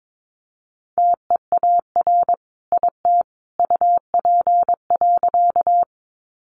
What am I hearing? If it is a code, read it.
Morse code, heard as a keyed tone, about 22 wpm, 700 Hz. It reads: TEAR IT UP.
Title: A Sound Of Thunder 22wpm